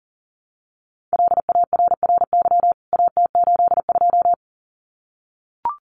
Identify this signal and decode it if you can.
Morse code, heard as a keyed tone, about 40 wpm, 700 Hz. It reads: LARRY AT82
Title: CWT Exchange 40wpm